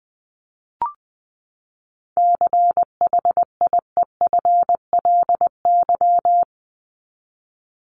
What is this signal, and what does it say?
Morse code, heard as a keyed tone, about 20 wpm, 700 Hz. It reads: CHIEFLY
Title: LY Suffix - Top 100 Words with Base Words 20wpm